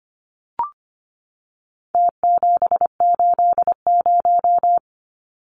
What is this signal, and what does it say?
Morse code, heard as a keyed tone, about 25 wpm, 700 Hz. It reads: T780
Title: Numbers - Contest Style - 4-Digit 25wpm